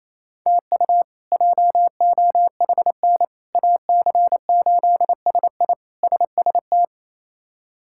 Morse code, heard as a keyed tone, about 28 wpm, 700 Hz. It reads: TU JOHN AC8SI SST
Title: K1USN SST QSO Practice - 28wpm